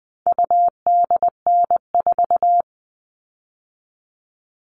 Morse code, heard as a keyed tone, about 20 wpm, 700 Hz. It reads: UDN4
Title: All Cut Numbers - 4-Digit 20wpm